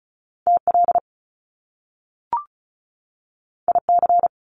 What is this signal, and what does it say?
Morse code, heard as a keyed tone, about 36 wpm, 700 Hz. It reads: TL IC
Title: Top Two Letter Combinations in Words 35wpm